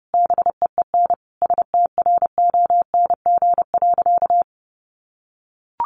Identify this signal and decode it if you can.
Morse code, heard as a keyed tone, about 30 wpm, 700 Hz. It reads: BEEN STRONG.
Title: Sentences from Top 1400 Words 30wpm